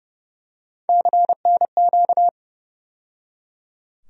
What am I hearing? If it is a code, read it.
Morse code, heard as a keyed tone, about 30 wpm, 700 Hz. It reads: CNQ
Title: USA Call Sign Postfixes - 3 Letters 30wpm